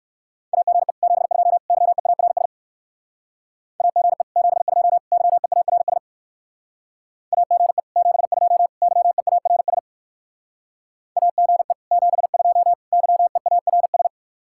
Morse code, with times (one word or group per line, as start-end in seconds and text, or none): 0.5-0.9 AGE
1.0-1.6 72
1.7-2.5 YEARS
3.8-4.2 AGE
4.4-5.0 72
5.1-6.0 YEARS
7.3-7.8 AGE
8.0-8.7 72
8.8-9.8 YEARS
11.2-11.7 AGE
11.9-12.7 72
12.9-14.1 YEARS